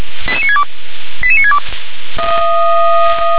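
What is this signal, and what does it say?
Five-tone selective call (ZVEI), heard as quick runs of five tones, tones of about 70 ms. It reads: 88962 88962
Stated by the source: ZVEI five-tone selective call received on SDR.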